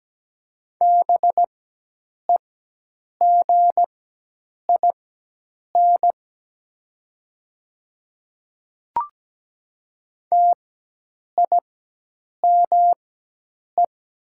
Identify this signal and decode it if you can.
Morse code, heard as a keyed tone, about 17 wpm, spaced as for 8 wpm, 700 Hz. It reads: BEGIN TIME
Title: 4x Character Spacing - Top 500 Words - 17wpm